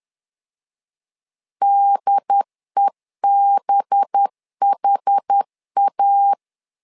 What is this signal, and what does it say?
Morse code, heard as a keyed tone, about 10 wpm, 800 Hz. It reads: DEBHA
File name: LEZIONI / 5 LBVC / 5 LBVC 10wpm 2349